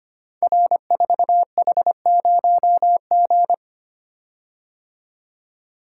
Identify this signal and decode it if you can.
Morse code, heard as a keyed tone, about 25 wpm, 700 Hz. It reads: R4H0G
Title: Code Groups - 5 Letter:Numbers - 25wpm